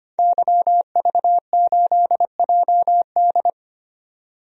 Morse code, heard as a keyed tone, about 25 wpm, 700 Hz. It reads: YV8JD
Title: DXCC-100 Call Signs - 5 Characters - 25wpm